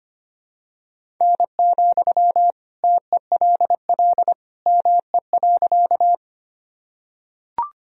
Morse code, heard as a keyed tone, about 25 wpm, 700 Hz. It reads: N, TELL ME.
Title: Sentences from Top 200 Words and Abbreviations 25wpm